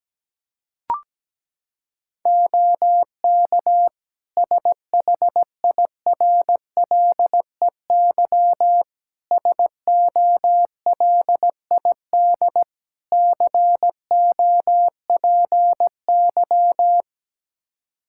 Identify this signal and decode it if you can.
Morse code, heard as a keyed tone, about 17 wpm, 700 Hz. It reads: OK SHIRLEY SOLID COPY
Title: Practice QSO - Exchange 3 - More Info - 17wpm